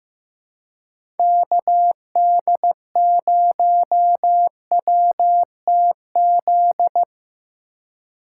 Morse code, heard as a keyed tone, about 15 wpm, 700 Hz. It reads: KD0WTZ
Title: USA Call Signs - 6 Characters - 2x Character Spacing - 15wpm